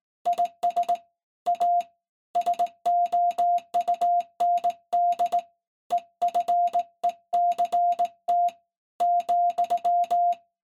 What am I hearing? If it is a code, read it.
Morse code, heard as a keyed tone, about 19 wpm, 700 Hz. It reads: IS A SOUND EFECT ,
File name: se / pre2 / Morse Code01-3(Fast-Key Noise)